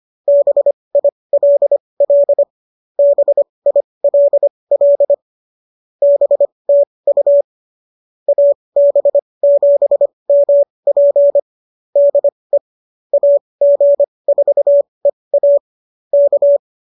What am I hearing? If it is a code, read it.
Morse code, heard as a keyed tone, about 25 wpm, spaced as for 20 wpm, 570 Hz. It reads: BILL BILL BTU AB7MP DE AG4EA K